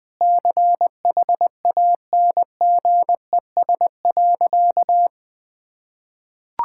Morse code, heard as a keyed tone, about 20 wpm, 700 Hz. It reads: CHANGES.
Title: Sentences with 6-Word Phrases - Spoken Beginning - 20wpm